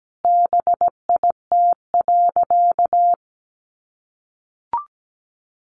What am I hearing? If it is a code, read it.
Morse code, heard as a keyed tone, about 17 wpm, 700 Hz. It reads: BIT.
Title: Sentences from Top 900 Words 17wpm